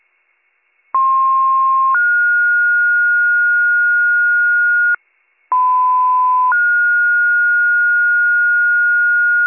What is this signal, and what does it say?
Two-tone page, heard as a one-second tone, then a three-second tone, about 1060 then 1430 Hz, about 1010 then 1430 Hz, with noise. It is heard from 0.9 to 9.5 s.